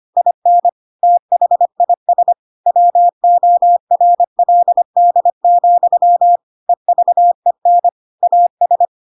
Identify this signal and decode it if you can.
Morse code, heard as a keyed tone, about 25 wpm, 700 Hz. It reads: IN THIS WORLD, EVEN AS
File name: Pinocchio / Pinocchio ch7 25 25 ik1hge